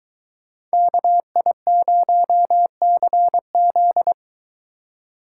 Morse code, heard as a keyed tone, about 23 wpm, 700 Hz. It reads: KI0CZ